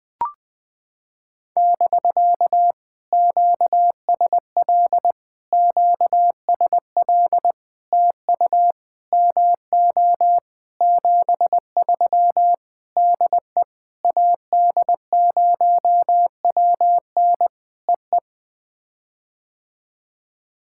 Morse code, heard as a keyed tone, about 19 wpm, 700 Hz. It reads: <BK> QSL QSL TU MO 73 DE AD0WN EE
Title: POTA QSO Practice - 20wpm